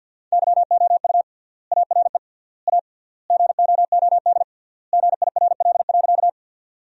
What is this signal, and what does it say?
Morse code, heard as a keyed tone, about 50 wpm, 700 Hz. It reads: YOU ARE A GOOD GIRL.
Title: 3x Word-Spacing - Sentences from Top 400 Words - 50wpm